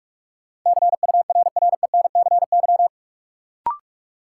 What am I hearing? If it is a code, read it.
Morse code, heard as a keyed tone, about 46 wpm, 700 Hz. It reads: CURRENCY
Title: Sentences from Top 1500 Words - New Word Drill - 45wpm